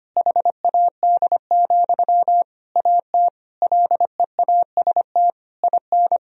Morse code, heard as a keyed tone, about 25 wpm, 700 Hz. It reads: HAD, AT LEAST IN
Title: Sentences from Top 1000 Words - Review Part 4 - 25wpm